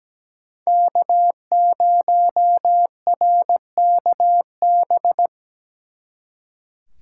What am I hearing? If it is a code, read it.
Morse code, heard as a keyed tone, about 20 wpm, 700 Hz. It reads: K0RKB
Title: Speed Racing - USA 1x3 Call Signs 17wpm